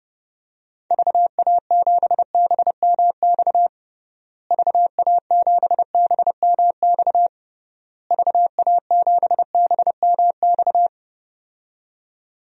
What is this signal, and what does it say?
Morse code, heard as a keyed tone, about 32 wpm, 700 Hz. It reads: VA7BMX VA7BMX VA7BMX
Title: DXCC-100 Call Signs - 6 Characters - Repeated 3x - 30wpm